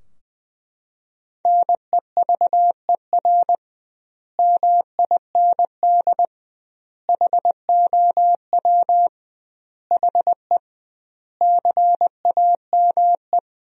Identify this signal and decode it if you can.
Morse code, heard as a keyed tone, about 20 wpm, 700 Hz. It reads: NEVER MIND HOW HE CAME
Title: 2x Word-Spacing - Sentences from Top 500 Words - 20wpm